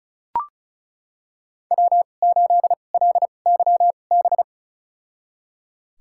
Morse code, heard as a keyed tone, about 36 wpm, spaced as for 26 wpm, 700 Hz. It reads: W8LYB
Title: USA Call Signs - 5 Characters - 2x Character Spacing - 35wpm